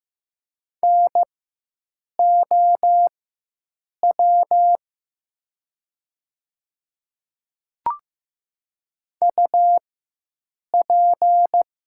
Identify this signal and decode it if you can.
Morse code, heard as a keyed tone, about 15 wpm, spaced as for 7 wpm, 700 Hz. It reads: NOW UP
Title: 4x Character Spacing - Top 100 Words - 15wpm